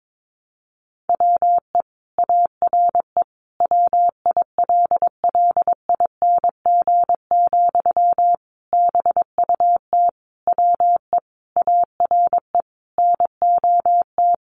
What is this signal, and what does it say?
Morse code, heard as a keyed tone, about 22 wpm, 700 Hz. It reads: WE ARE WILLING, BUT WE ARE NOT
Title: Sentences from Top 1300 Words 22wpm